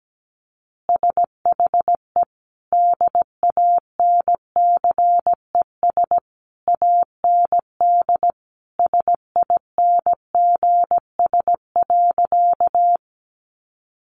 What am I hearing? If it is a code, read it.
Morse code, heard as a keyed tone, about 17 wpm, 700 Hz. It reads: SHE DANCES AND SINGS.
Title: Easy Phrases 17wpm